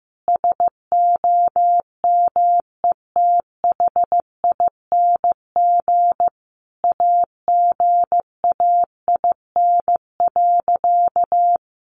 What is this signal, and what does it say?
Morse code, heard as a keyed tone, about 15 wpm, 700 Hz. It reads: SOMETHING AGAIN.
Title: Sentences from Top 1100 Words 15wpm